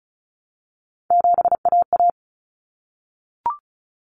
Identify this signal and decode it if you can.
Morse code, heard as a keyed tone, about 34 wpm, 700 Hz. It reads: 7AA
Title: Common Cut Numbers - 3-Digit 35wpm